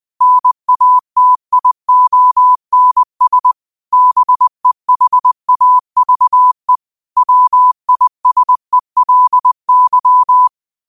Morse code, heard as a keyed tone, about 20 wpm, 1000 Hz. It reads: NATIONS BEHAVE WISELY
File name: qotdfiles / 20wpm